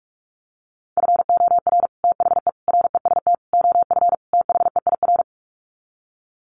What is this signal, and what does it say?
Morse code, heard as a keyed tone, about 44 wpm, 700 Hz. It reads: FOR THE REST OF THEIR
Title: Sets of 5 Words Encore2 45wpm